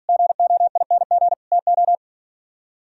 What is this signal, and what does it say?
Morse code, heard as a keyed tone, about 56 wpm, 700 Hz. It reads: GOING TO